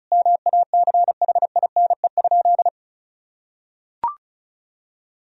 Morse code, heard as a keyed tone, about 34 wpm, 700 Hz. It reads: MACHINE?
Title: Sentences from Top 1000 Words - Review Part 2 - 35wpm